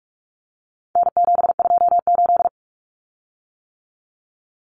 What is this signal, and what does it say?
Morse code, heard as a keyed tone, about 44 wpm, 700 Hz. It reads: N728
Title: Common Cut Numbers - 4-Digit 45wpm